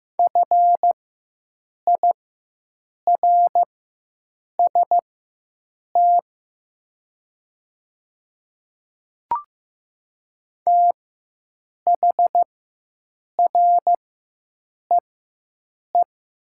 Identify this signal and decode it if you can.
Morse code, heard as a keyed tone, about 15 wpm, spaced as for 7 wpm, 700 Hz. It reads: FIRST THREE
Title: 4x Character Spacing - Top 200 Words - 15wpm